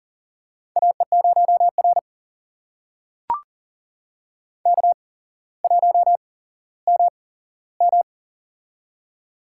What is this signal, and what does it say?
Morse code, heard as a keyed tone, about 38 wpm, spaced as for 17 wpm, 700 Hz. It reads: AE0R K1MM
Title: USA Call Signs - 4 Characters - 8x Character Spacing - 40wpm